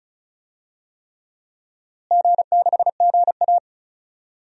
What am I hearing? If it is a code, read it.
Morse code, heard as a keyed tone, about 36 wpm, 700 Hz. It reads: G6GA
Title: DXCC-100 Call Signs - 4 Characters - Repeated 3x - 35wpm